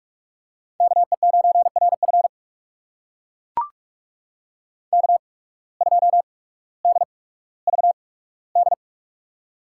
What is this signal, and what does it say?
Morse code, heard as a keyed tone, about 44 wpm, spaced as for 19 wpm, 700 Hz. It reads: KE9RF K2DVD
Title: USA Call Signs - 5 Characters - 8x Character Spacing - 45wpm